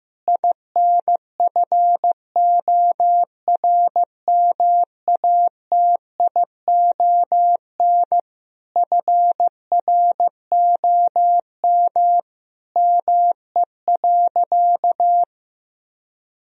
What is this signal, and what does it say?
Morse code, heard as a keyed tone, about 15 wpm, 700 Hz. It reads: INFORMATION FROM ME.
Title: Sentences from Top 1000 Words - Review Part 3 - 15wpm